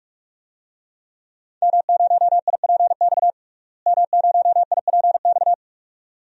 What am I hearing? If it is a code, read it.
Morse code, heard as a keyed tone, about 41 wpm, 700 Hz. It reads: M0IPX M0IPX
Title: DXCC-100 Call Signs - 5 Characters - Repeated 2x - 45wpm